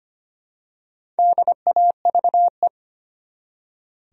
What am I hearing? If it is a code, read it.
Morse code, heard as a keyed tone, about 25 wpm, 700 Hz. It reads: DAVE
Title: USA Common Names - Max 4 Letters - 25wpm